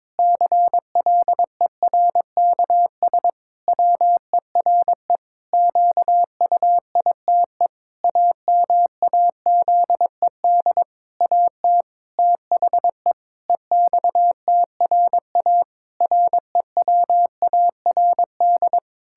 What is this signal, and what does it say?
Morse code, heard as a keyed tone, about 22 wpm, 700 Hz. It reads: CLERKS WERE QUITE AMAZED AT THE EXTRA REWARD
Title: Pangram Challenge v2 - 22wpm